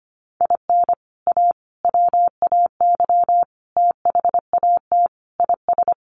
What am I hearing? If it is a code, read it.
Morse code, heard as a keyed tone, about 25 wpm, 700 Hz. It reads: IN A WAY THAT IS
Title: Sets of 5 Words Encore 25wpm